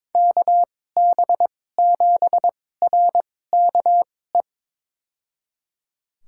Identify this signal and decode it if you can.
Morse code, heard as a keyed tone, about 22 wpm, spaced as for 16 wpm, 700 Hz. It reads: KB7RKE
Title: USA Call Signs - 2x Character Spacing - 22wpm